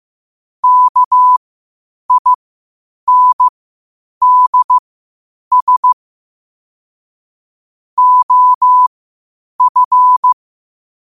Morse code, heard as a keyed tone, about 15 wpm, spaced as for 8 wpm, 1000 Hz. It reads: KINDS OF
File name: qotdfiles / 8wpm